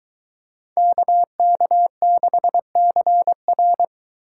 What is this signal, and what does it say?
Morse code, heard as a keyed tone, about 22 wpm, 700 Hz. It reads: KK6CR